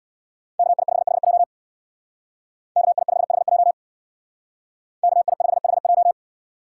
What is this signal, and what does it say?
Morse code, heard as a keyed tone, about 62 wpm, 700 Hz. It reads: KI5HJ KI5HJ KI5HJ
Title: Speed Racing - USA 2x2 Call Signs 50wpm